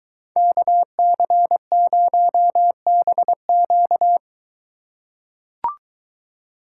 Morse code, heard as a keyed tone, about 23 wpm, 700 Hz. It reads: KC0BQ